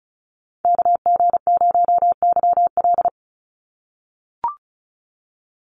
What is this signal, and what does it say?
Morse code, heard as a keyed tone, about 34 wpm, 700 Hz. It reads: KG0YL
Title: Call Signs - 35wpm